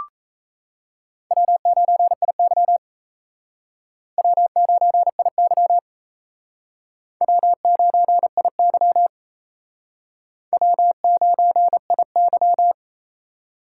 Morse code, heard as a keyed tone, about 34 wpm, 700 Hz. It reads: W9IY W9IY W9IY W9IY
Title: Speed Racing - USA 1x2 Call Signs 28wpm